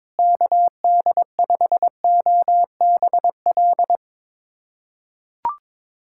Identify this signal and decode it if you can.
Morse code, heard as a keyed tone, about 21 wpm, 700 Hz. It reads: KD5OBL
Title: USA Callsign DE Callsign 22wpm